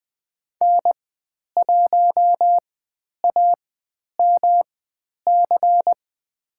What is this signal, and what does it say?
Morse code, heard as a keyed tone, about 20 wpm, spaced as for 10 wpm, 700 Hz. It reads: N1AMC
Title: 109 Lesson 18g - Letter m - Call Signs 20wpm